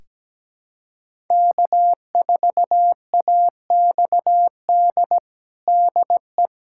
Morse code, heard as a keyed tone, about 17 wpm, 700 Hz. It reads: K4AXD DE
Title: USA Callsign DE Callsign 17wpm